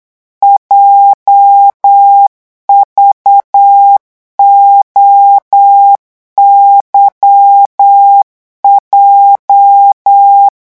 Morse code, heard as a keyed tone, about 8 wpm, 800 Hz. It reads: JVOYJ
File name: morse mp3 / revision8-vitesse 8 mots